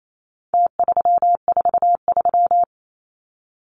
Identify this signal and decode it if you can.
Morse code, heard as a keyed tone, about 28 wpm, 700 Hz. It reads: T343
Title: Numbers - Contest Style - 4-Digit 28wpm